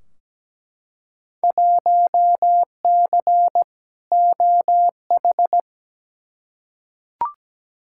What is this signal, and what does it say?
Morse code, heard as a keyed tone, about 17 wpm, 700 Hz. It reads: 1C OH
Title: Field Day Exchanges - 17wpm